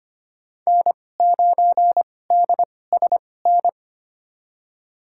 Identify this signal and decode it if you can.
Morse code, heard as a keyed tone, about 25 wpm, spaced as for 18 wpm, 700 Hz. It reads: N9DSN
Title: USA Call Signs - 5 Characters - 2x Character Spacing - 25wpm